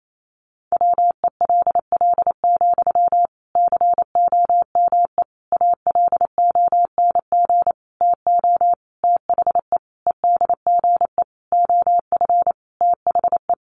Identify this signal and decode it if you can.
Morse code, heard as a keyed tone, about 28 wpm, 700 Hz. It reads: WELL, COME ALONG TO THE EDGE OF THE
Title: Sentences from Top 1300 Words 28wpm